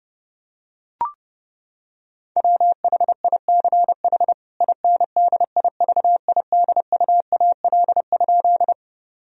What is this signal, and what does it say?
Morse code, heard as a keyed tone, about 30 wpm, 700 Hz. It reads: WHICH INDIVIDUAL?
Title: Sentences from Top 1000 Words - Review Part 4 - 30wpm